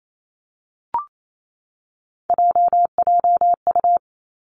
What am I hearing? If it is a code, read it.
Morse code, heard as a keyed tone, about 28 wpm, 700 Hz. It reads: JJU
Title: Three Challenging Letter-Number Combinations 28wpm